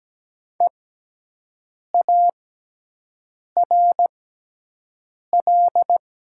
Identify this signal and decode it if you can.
Morse code, heard as a keyed tone, about 17 wpm, spaced as for 6 wpm, 700 Hz. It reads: EARL
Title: 6x Character Spacing - Top 500 Words - 17wpm